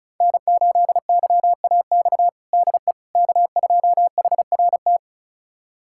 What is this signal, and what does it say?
Morse code, heard as a keyed tone, about 36 wpm, 700 Hz. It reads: N8YAX DE K2HRT
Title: USA Callsign DE Callsign - 5 Characters - 35wpm